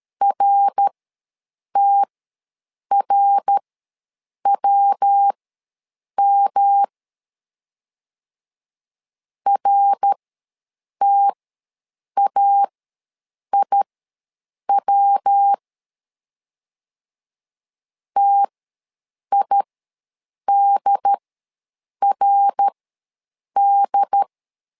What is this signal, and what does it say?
Morse code, heard as a keyed tone, about 13 wpm, spaced as for 7 wpm, 800 Hz. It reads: RTRWM RTAIW TIDRD
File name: LEZIONI / 3 MWRO / 3 MWR ws 12wpm 1809